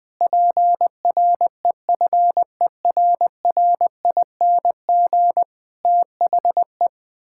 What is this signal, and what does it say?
Morse code, heard as a keyed tone, about 19 wpm, 700 Hz. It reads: PREFERRING THE
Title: Pangram Challenge v2 - 20wpm